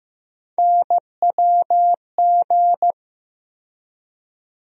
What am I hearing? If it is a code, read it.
Morse code, heard as a keyed tone, about 15 wpm, 700 Hz. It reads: NWG
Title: USA Call Sign Postfixes - 3 Letters 15wpm